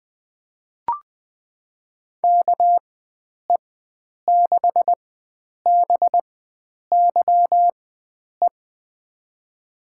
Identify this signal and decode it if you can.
Morse code, heard as a keyed tone, about 20 wpm, spaced as for 9 wpm, 700 Hz. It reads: KE6BYE
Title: USA Call Signs - 6 Characters - 4x Character Spacing - 20wpm